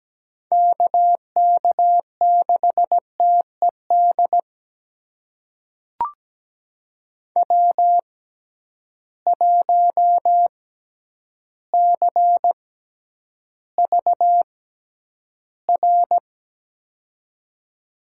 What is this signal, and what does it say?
Morse code, heard as a keyed tone, about 17 wpm, spaced as for 9 wpm, 700 Hz. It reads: KK6TED W1CVR
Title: USA Call Signs - 6x Character Spacing - 17wpm